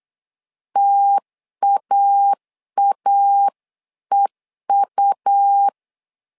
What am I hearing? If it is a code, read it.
Morse code, heard as a keyed tone, about 8 wpm, 800 Hz. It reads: TAAEU
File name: LEZIONI / 3 MWRO / 3 M 8wpm 2300